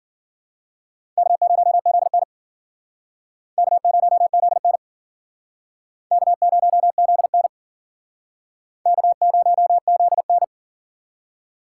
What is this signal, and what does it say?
Morse code, heard as a keyed tone, about 53 wpm, 700 Hz. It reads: K0ZN K0ZN K0ZN K0ZN